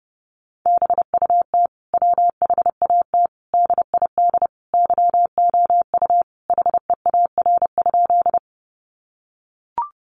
Morse code, heard as a keyed tone, about 30 wpm, 700 Hz. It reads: BUT WHAT DID YOU HEAR?
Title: Sentences from Top 300 Words 30wpm